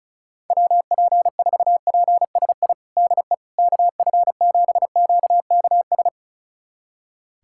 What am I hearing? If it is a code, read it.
Morse code, heard as a keyed tone, about 36 wpm, 700 Hz. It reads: WP4PSI DE KF7QKS
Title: USA Callsign DE Callsign - 6 Characters - 35wpm